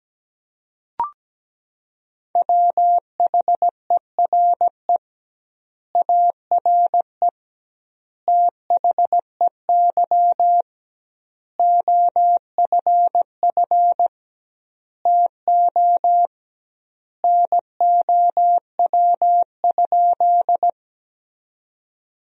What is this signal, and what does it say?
Morse code, heard as a keyed tone, about 17 wpm, 700 Hz. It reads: WHERE ARE THEY OFF TO NOW?
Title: 2x Word-Spacing - Sentences from Top 200 Words - 17wpm